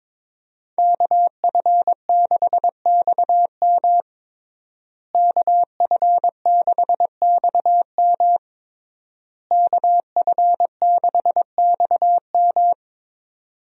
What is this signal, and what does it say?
Morse code, heard as a keyed tone, about 22 wpm, 700 Hz. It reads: KF6XM KF6XM KF6XM
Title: USA Call Signs - 5 Characters - Repeated 3x - 22wpm